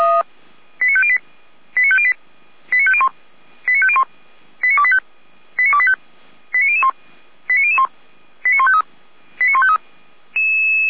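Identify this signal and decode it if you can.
Five-tone selective call (ZVEI), heard as quick runs of five tones, tones of about 70 ms. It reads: 89508 89508 89501 89501 89285 89285 89001 89001 89153 89153
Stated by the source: ZVEI five-tone call received by SDR.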